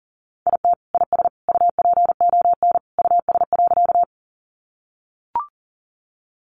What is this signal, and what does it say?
Morse code, heard as a keyed tone, about 40 wpm, 700 Hz. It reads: IT IS UPON US.
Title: Sentences from Top 1100 Words 40wpm